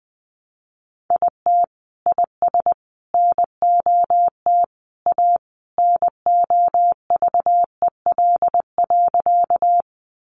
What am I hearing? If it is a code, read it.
Morse code, heard as a keyed tone, about 20 wpm, 700 Hz. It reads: IT IS NOT A NOVEL.
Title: Sentences from Top 1300 Words 20wpm